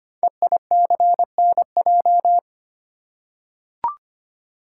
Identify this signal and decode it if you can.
Morse code, heard as a keyed tone, about 25 wpm, 690 Hz. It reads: EICNJ
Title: Code Groups - 5 Letters - 25wpm